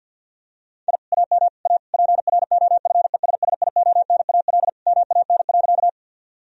Morse code, heard as a keyed tone, about 50 wpm, 700 Hz. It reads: I AM A PROFESSIONAL MAN.
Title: Sentences from Top 1000 Words - Review Part 4 - 50wpm